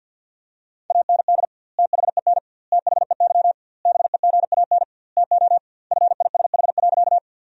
Morse code, heard as a keyed tone, about 50 wpm, 700 Hz. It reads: AND THEN THEY BEGAN TO FISH.
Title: 2x Word-Spacing - Sentences from Top 300 Words - 50wpm